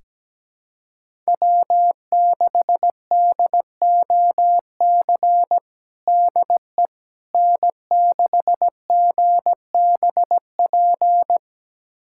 Morse code, heard as a keyed tone, about 17 wpm, 700 Hz. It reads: W6DOC DE N6GBP
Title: USA Callsign DE Callsign - 5 Characters - 17wpm